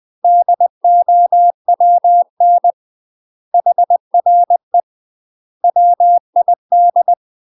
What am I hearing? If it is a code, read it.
Morse code, heard as a keyed tone, about 20 wpm, 700 Hz. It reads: DOWN HRE WID
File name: UK MASTER QSOs TRAINING / 20WPM / UK MASTER QSOs 0028